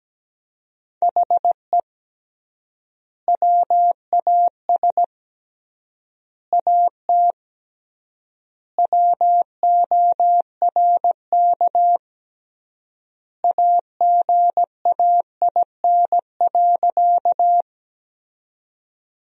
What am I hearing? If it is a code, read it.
Morse code, heard as a keyed tone, about 17 wpm, 700 Hz. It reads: HE WAS AT WORK AGAIN.
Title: 3x Word-Spacing - Sentences from Top 200 Words - 17wpm